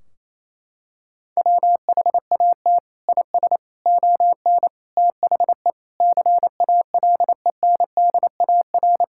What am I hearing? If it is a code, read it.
Morse code, heard as a keyed tone, about 28 wpm, 700 Hz. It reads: WHAT IS ON THE CALENDAR
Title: Sentences from Top 1600 Words 28wpm